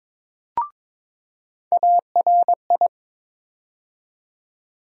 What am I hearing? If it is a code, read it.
Morse code, heard as a keyed tone, about 22 wpm, 700 Hz. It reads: ARI
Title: USA Call Sign Postfixes - 3 Letters 22wpm